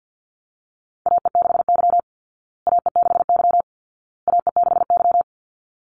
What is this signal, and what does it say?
Morse code, heard as a keyed tone, about 50 wpm, 700 Hz. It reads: AE6Y AE6Y AE6Y
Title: Call Signs Repeated 2x - 50wpm